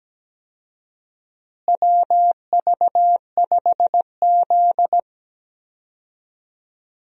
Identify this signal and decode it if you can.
Morse code, heard as a keyed tone, about 17 wpm, 700 Hz. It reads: WV5Z
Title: USA Call Signs - 4 Characters - Repeated 3x - 17wpm